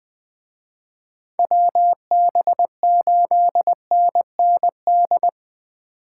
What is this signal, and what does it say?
Morse code, heard as a keyed tone, about 23 wpm, 700 Hz. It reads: WB8NND
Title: Speed Racing - USA Call Signs 20wpm